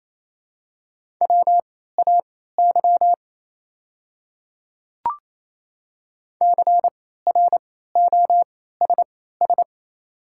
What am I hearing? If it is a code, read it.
Morse code, heard as a keyed tone, about 28 wpm, spaced as for 16 wpm, 700 Hz. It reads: WAY CROSS
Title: 3x Character Spacing - Top 300 Words - 28wpm